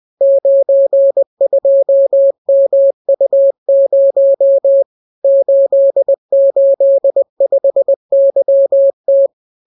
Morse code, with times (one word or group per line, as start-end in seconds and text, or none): 0.2-4.8 92MU0
5.2-9.3 885YT